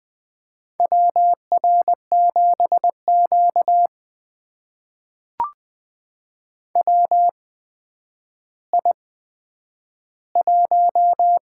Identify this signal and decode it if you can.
Morse code, heard as a keyed tone, about 20 wpm, spaced as for 10 wpm, 700 Hz. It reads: WR7Q WI1
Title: USA Call Signs - 4 Characters - 8x Character Spacing - 20wpm